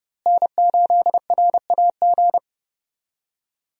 Morse code, heard as a keyed tone, about 32 wpm, 700 Hz. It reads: N8RAG
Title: USA Call Signs - 5 Characters - Repeated 2x - 30wpm